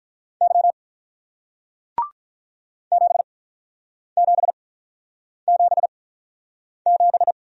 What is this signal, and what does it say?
Morse code, heard as a keyed tone, about 48 wpm, spaced as for 9 wpm, 700 Hz. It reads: Y 7777